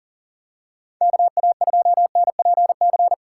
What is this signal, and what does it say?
Morse code, heard as a keyed tone, about 39 wpm, 700 Hz. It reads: KA2NPC
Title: USA Call Signs - 6 Characters - 2x Character Spacing - 40wpm